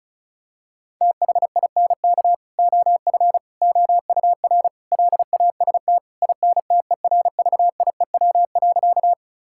Morse code, heard as a keyed tone, about 34 wpm, 700 Hz. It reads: THINK OF OUR LAST INTERVIEW.
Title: Sentences from Top 1600 Words 35wpm